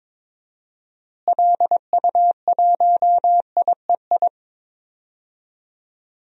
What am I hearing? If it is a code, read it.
Morse code, heard as a keyed tone, about 21 wpm, 700 Hz. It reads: LU1IEI
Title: DXCC-100 Call Signs - 6 Characters - Repeated 2x - 22wpm